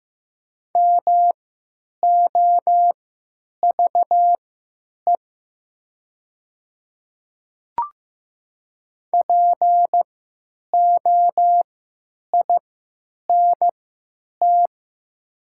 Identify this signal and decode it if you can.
Morse code, heard as a keyed tone, about 15 wpm, spaced as for 9 wpm, 700 Hz. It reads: MOVE POINT
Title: 3x Character Spacing - Top 500 Words - 15wpm